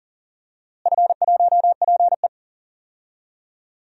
Morse code, heard as a keyed tone, about 40 wpm, 700 Hz. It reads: F1PE
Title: DXCC-100 Call Signs - 4 Characters - Repeated 3x - 40wpm